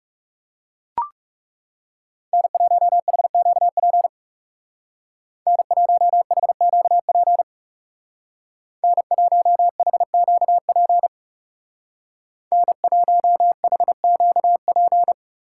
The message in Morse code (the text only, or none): N1HQP N1HQP N1HQP N1HQP